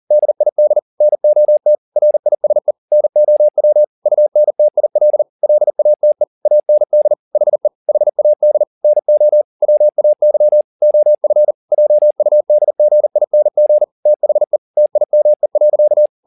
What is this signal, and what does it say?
Morse code, heard as a keyed tone, about 40 wpm, 600 Hz. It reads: DID NOT RISE NOW UNTIL LATE AND HE HAD NO WAY OF JUDGING THE TIME.
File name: the old man and the sea 2 40wpm